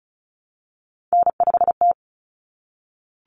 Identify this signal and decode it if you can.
Morse code, heard as a keyed tone, about 34 wpm, 700 Hz. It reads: N5T
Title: Common Cut Numbers - 3-Digit 35wpm